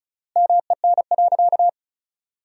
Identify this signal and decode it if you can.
Morse code, heard as a keyed tone, about 36 wpm, 700 Hz. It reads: MEN.
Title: 3x Word-Spacing - Sentences from Top 300 Words - 35wpm